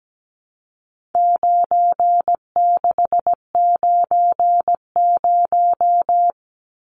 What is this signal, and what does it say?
Morse code, heard as a keyed tone, about 17 wpm, 700 Hz. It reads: 9690
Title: Numbers - 4-Digit 17wpm